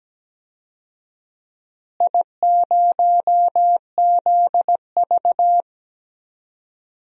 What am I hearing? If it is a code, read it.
Morse code, heard as a keyed tone, about 17 wpm, 700 Hz. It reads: I0ZV
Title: DXCC-100 Call Signs - 4 Characters - Repeated 2x - 17wpm